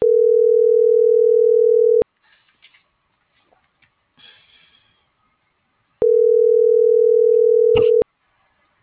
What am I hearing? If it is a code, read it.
Ringback tone (North America), 440 + 480 Hz.